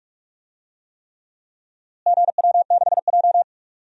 Morse code, heard as a keyed tone, about 46 wpm, 700 Hz. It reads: GW6J
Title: DXCC-100 Call Signs - 4 Characters - Repeated 3x - 45wpm